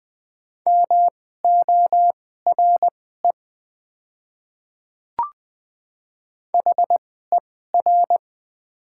Morse code, heard as a keyed tone, about 20 wpm, spaced as for 14 wpm, 700 Hz. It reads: MORE HER
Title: 2x Character Spacing - Top 100 Words - 20wpm